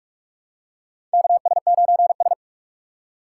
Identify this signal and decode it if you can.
Morse code, heard as a keyed tone, about 46 wpm, 700 Hz. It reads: KS9S
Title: USA Call Signs - 4 Characters - 4x Character Spacing - 45wpm